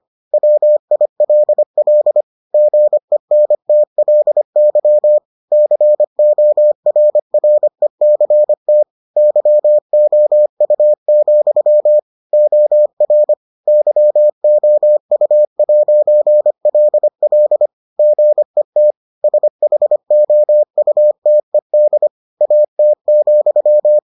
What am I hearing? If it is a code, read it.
Morse code, heard as a keyed tone, about 25 wpm, 600 Hz. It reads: WILL GENTLY CORRECT YOU, OR YOU'LL GET SHOUTED AT,